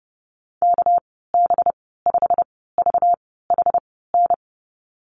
Morse code, heard as a keyed tone, about 30 wpm, spaced as for 17 wpm, 700 Hz. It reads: KB5VHN
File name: Lesson 37g - Letter x - Call Signs 30wpm